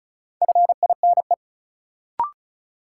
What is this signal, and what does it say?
Morse code, heard as a keyed tone, about 34 wpm, 700 Hz. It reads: FINE
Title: Sentences from Top 500 Words - New Word Drill - 35wpm